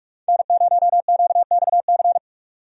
Morse code, heard as a keyed tone, about 46 wpm, 700 Hz. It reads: N0QXC
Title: USA Call Signs - 5 Characters - 45wpm